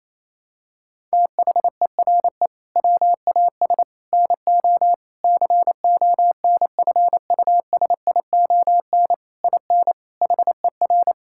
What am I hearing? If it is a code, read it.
Morse code, heard as a keyed tone, about 28 wpm, 700 Hz. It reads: THERE WAS NO CONFUSION IN HER
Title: Sentences from Top 1600 Words 28wpm